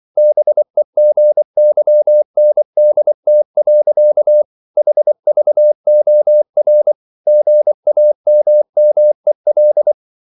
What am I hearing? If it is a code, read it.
Morse code, heard as a keyed tone, about 24 wpm, 600 Hz. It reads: BEGYNDT. HVOR GAMMEL
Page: Ferie 24 wpm